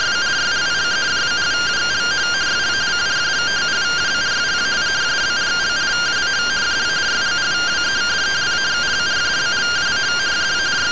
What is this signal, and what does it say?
Two-tone teleprinter signal, about 170 Hz shift.